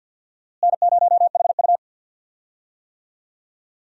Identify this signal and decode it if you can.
Morse code, heard as a keyed tone, about 52 wpm, 700 Hz. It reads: N0HU
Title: USA Call Signs - 4 Characters - Repeated 3x - 50wpm